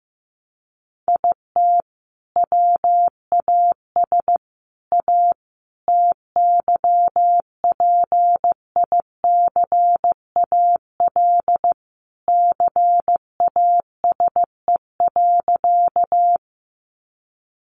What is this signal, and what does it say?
Morse code, heard as a keyed tone, about 15 wpm, 700 Hz. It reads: IT WAS A TYPICAL CASE.
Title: Sentences from Top 1100 Words 15wpm